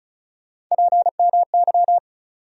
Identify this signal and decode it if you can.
Morse code, heard as a keyed tone, about 34 wpm, 700 Hz. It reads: PMY
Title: USA Call Sign Postfixes - 3 Letters 35wpm